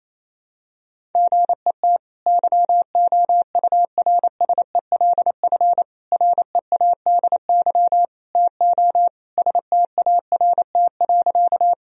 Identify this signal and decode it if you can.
Morse code, heard as a keyed tone, about 28 wpm, 700 Hz. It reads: GET YOURSELF READY TO START.
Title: Sentences from Top 1000 Words - Review Part 1 - 28wpm